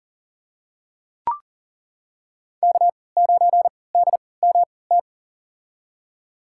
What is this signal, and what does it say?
Morse code, heard as a keyed tone, about 40 wpm, spaced as for 23 wpm, 690 Hz. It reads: K9DMT
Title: USA Call Signs - 5 Characters - 3x Character Spacing - 40wpm